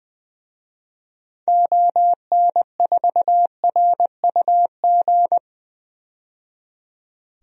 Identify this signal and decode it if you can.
Morse code, heard as a keyed tone, about 20 wpm, 700 Hz. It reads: ON4RUG
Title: DXCC-100 Call Signs - 6 Characters - 20wpm